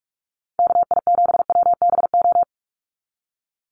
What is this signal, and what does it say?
Morse code, heard as a keyed tone, about 44 wpm, 700 Hz. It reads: KI7WBO
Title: USA Call Signs 45wpm